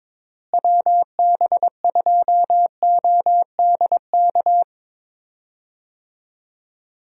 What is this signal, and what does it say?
Morse code, heard as a keyed tone, about 22 wpm, 700 Hz. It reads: WB2ODK
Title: USA Call Signs - 6 Characters - Repeated 2x - 22wpm